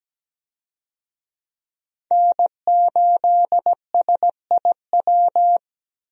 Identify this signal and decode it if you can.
Morse code, heard as a keyed tone, about 17 wpm, 700 Hz. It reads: N8SIW
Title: USA Call Signs - 5 Characters - 17wpm